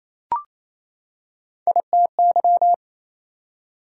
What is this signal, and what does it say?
Morse code, heard as a keyed tone, about 28 wpm, 700 Hz. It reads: ITY
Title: Prefixes and Suffixes 28wpm